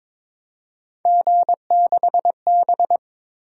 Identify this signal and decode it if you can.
Morse code, heard as a keyed tone, about 22 wpm, 700 Hz. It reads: G6B
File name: practice / 22wpm / Three Letter-Number Combinations - 22wpm